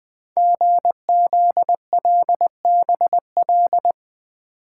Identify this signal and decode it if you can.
Morse code, heard as a keyed tone, about 20 wpm, 700 Hz. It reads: GZLBL
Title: Code Groups - 5 Letters - 20wpm